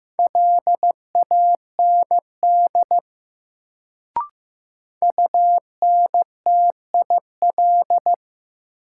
Morse code, heard as a keyed tone, about 15 wpm, 700 Hz. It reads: LAND UNTIL
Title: Sentences from Top 500 Words - New Word Drill - 15wpm